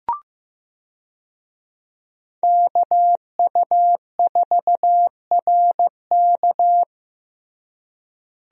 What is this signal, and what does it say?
Morse code, heard as a keyed tone, about 16 wpm, 700 Hz. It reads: KU4RK
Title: USA Call Signs - 5 Characters - Repeated 2x - 15wpm